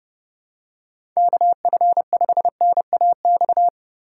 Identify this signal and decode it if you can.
Morse code, heard as a keyed tone, about 30 wpm, 700 Hz. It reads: KF5NAX
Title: USA Call Signs - 6 Characters - 4x Character Spacing - 30wpm